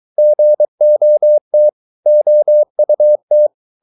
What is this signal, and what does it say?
Morse code, heard as a keyed tone, about 23 wpm, 600 Hz. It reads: GOT OUT
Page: Romanen Never af Ken Follet på engelsk.